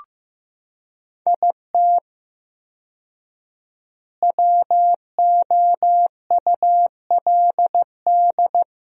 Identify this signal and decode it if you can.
Morse code, heard as a keyed tone, about 15 wpm, 700 Hz. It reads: IT WOULD
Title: 4x Word-Spacing - Sentences from Top 400 Words - 15wpm